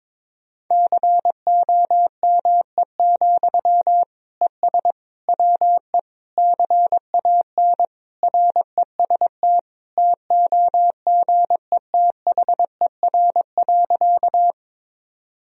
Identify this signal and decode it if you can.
Morse code, heard as a keyed tone, about 22 wpm, 700 Hz. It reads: COME, ES WE CAN REST TOGETHER.
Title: Sentences from Top 500 Words and Abbreviations 22wpm